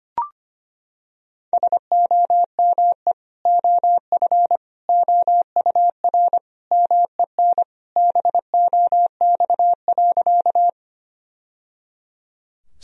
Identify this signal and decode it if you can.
Morse code, heard as a keyed tone, about 25 wpm, 700 Hz. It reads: SOME OF OUR MEN BOX.
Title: Sentences from Top 1000 Words - Review Part 2 - 25wpm